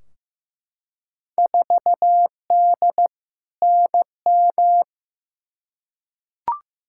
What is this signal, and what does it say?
Morse code, heard as a keyed tone, about 15 wpm, 700 Hz. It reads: 4D NM
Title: Field Day Exchanges Encore - 15wpm